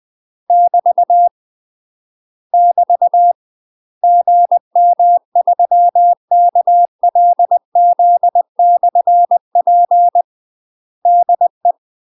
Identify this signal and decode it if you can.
Morse code, heard as a keyed tone, about 20 wpm, 700 Hz. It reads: <BT> <BT> GM3KLZ/P DE
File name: UK MASTER QSOs TRAINING / 20WPM / UK MASTER QSOs 0067